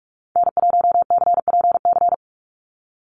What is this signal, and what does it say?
Morse code, heard as a keyed tone, about 44 wpm, 700 Hz. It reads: N1CPC
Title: USA Call Signs 45wpm